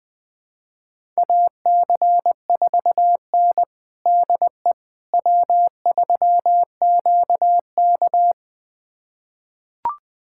Morse code, heard as a keyed tone, about 20 wpm, 700 Hz. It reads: AC4N DE W3QK
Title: USA Callsign DE Callsign - 4 Characters - 20wpm